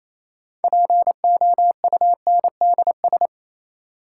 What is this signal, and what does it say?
Morse code, heard as a keyed tone, about 28 wpm, 700 Hz. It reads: POUNDS
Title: Sentences from Top 1000 Words - Review Part 2 - New Word Drill - 28wpm